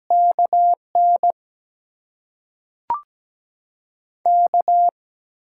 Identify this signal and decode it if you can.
Morse code, heard as a keyed tone, about 17 wpm, 700 Hz. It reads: KN K
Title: USA Call Signs - 6 Characters - 8x Character Spacing - 17wpm